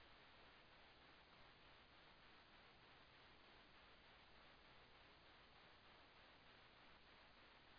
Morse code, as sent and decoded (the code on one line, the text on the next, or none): none